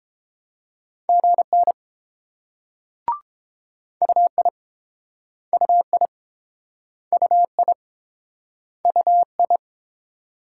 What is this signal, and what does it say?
Morse code, heard as a keyed tone, about 30 wpm, 700 Hz. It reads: GN UI UI UI UI